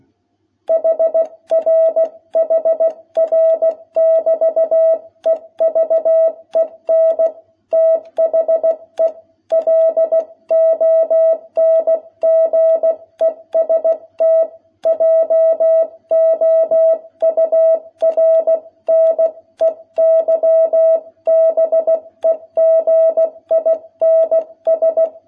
Morse code, heard as a keyed tone, about 16 wpm, 660 Hz. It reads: HR HR <BT> EVEN THE LONGEST JOURNEY BEGINS
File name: jh8cbh / 2021cw